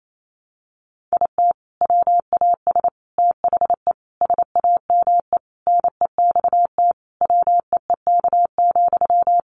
Morse code, heard as a keyed tone, about 28 wpm, 700 Hz. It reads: IT WAS THE SAME NEXT WEEK,
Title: Sentences from Top 500 Words 28wpm